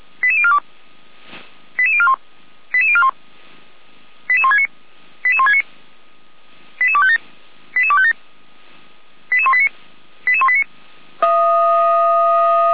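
Five-tone selective call (ZVEI), heard as quick runs of five tones, tones of about 70 ms. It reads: 88042 88041 88041 88169 88169 88257 88257 88189 88189
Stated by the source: ZVEI five-tone selective call 88191, received by SDR.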